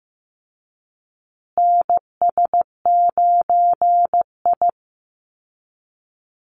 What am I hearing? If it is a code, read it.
Morse code, heard as a keyed tone, about 15 wpm, 700 Hz. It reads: NS9I
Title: Call Signs Repeated 2x - 15wpm